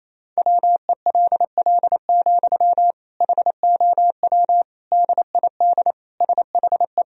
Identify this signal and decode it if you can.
Morse code, heard as a keyed tone, about 28 wpm, 700 Hz. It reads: WELL, HOW DID SHE
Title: Sentences from Top 1000 Words - Review Part 2 - 28wpm